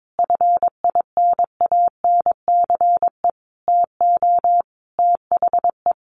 Morse code, heard as a keyed tone, about 22 wpm, 700 Hz. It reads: FINANCE TO THE
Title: Sentences from Top 1500 Words 22wpm